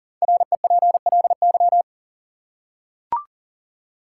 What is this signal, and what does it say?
Morse code, heard as a keyed tone, about 40 wpm, 700 Hz. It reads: REPLY
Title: Five Letter Words - 40wpm